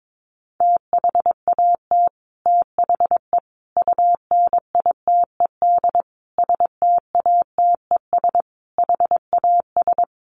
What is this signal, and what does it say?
Morse code, heard as a keyed tone, about 22 wpm, 700 Hz. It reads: THAT THE UNITED STATES HAS
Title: Sets of 5 Words Encore 22wpm